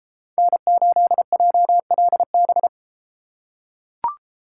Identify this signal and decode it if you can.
Morse code, heard as a keyed tone, about 33 wpm, 700 Hz. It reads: N8JLB